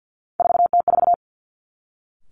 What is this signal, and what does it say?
Morse code, heard as a keyed tone, about 50 wpm, 700 Hz. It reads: 4T4
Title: Three Challenging Letter-Number Combinations 50wpm